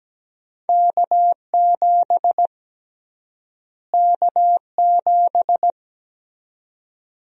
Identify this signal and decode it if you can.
Morse code, heard as a keyed tone, about 17 wpm, 700 Hz. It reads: K7 K7
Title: USA Call Sign Prefixes - Repeated 2x - 17wpm